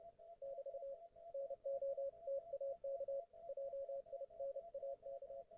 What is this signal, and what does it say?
Morse code, heard as a keyed tone, about 30 wpm, 550 Hz, with radio noise. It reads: <BT> NOTAK JINAK